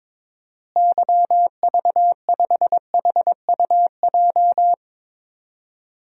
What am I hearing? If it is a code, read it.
Morse code, heard as a keyed tone, about 22 wpm, 700 Hz. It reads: YV5HUJ
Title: DXCC-100 Call Signs - 6 Characters - Repeated 2x - 22wpm